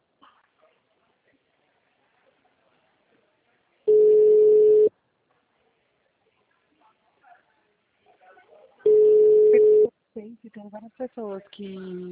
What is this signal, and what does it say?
Ringback tone (Europe, 425 Hz), with speech.